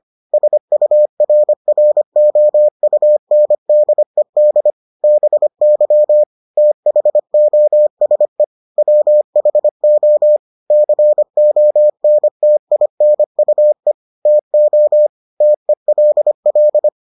Morse code, heard as a keyed tone, about 25 wpm, 600 Hz. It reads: SURROUNDED BY THOSE WHO CONTINUE TO TELL